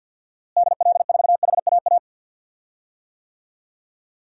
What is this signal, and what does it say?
Morse code, heard as a keyed tone, about 53 wpm, 700 Hz. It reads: DL4HAA
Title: DXCC-100 Call Signs - 6 Characters - Repeated 3x - 50wpm